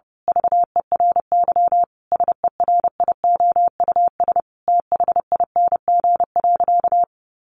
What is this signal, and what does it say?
Morse code, heard as a keyed tone, about 30 wpm, 700 Hz. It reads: VERY SERIOUS THING.
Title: Sentences from Top 1500 Words 30wpm